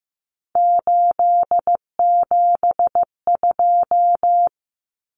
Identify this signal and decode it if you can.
Morse code, heard as a keyed tone, about 15 wpm, 700 Hz. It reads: 872
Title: Common Cut Numbers - 3-Digit 15wpm